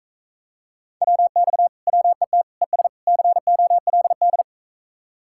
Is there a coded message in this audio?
Yes: Morse code from 1.0 to 4.4 s, heard as a keyed tone, about 45 wpm, 700 Hz. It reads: WX WET ES COLD